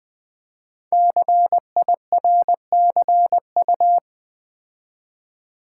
Morse code, heard as a keyed tone, about 20 wpm, 700 Hz. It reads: CIRCU
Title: Build A Word - 5 Letters At A Time - 12-Letter Words - 20wpm